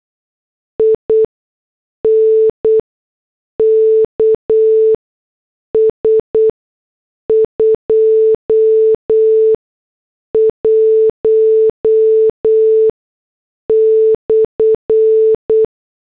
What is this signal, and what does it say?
Morse code, heard as a keyed tone, about 8 wpm, spaced as for 6 wpm, 440 Hz. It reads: INKS 21/